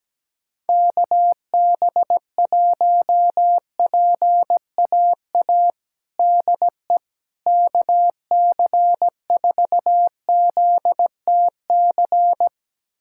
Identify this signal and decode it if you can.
Morse code, heard as a keyed tone, about 17 wpm, 700 Hz. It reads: KB1PAA DE KC4ZTC
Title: USA Callsign DE Callsign - 6 Characters - 17wpm